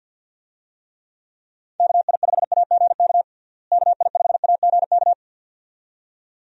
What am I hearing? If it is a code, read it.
Morse code, heard as a keyed tone, about 53 wpm, 700 Hz. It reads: KI5AGK KI5AGK
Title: USA Call Signs - 6 Characters - Repeated 2x - 50wpm